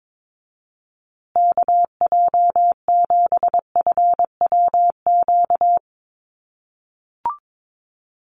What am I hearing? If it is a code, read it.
Morse code, heard as a keyed tone, about 22 wpm, 700 Hz. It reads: KJ7FWQ
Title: USA Call Signs Repeated 2x 22wpm